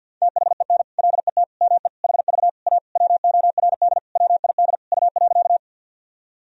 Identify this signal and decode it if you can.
Morse code, heard as a keyed tone, about 52 wpm, 700 Hz. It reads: THEN LET ME HV A WORD WID U.
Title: Sentences from Top 300 Words and Abbreviations 50wpm